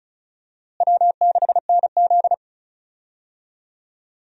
Morse code, heard as a keyed tone, about 34 wpm, 700 Hz. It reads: W6NZ
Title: USA Call Signs - 4 Characters - Repeated 2x - 35wpm